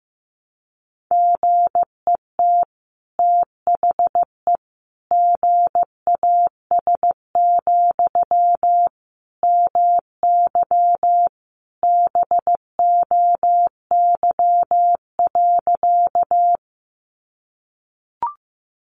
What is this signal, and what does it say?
Morse code, heard as a keyed tone, about 15 wpm, 700 Hz. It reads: GET THE GAS, MY BOY.
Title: Sentences from Top 1100 Words 15wpm